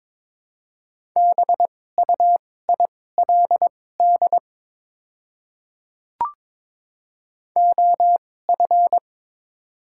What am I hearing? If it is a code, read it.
Morse code, heard as a keyed tone, about 22 wpm, spaced as for 16 wpm, 700 Hz. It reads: BUILD OF
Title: 2x Character Spacing - Top 300 Words - 22wpm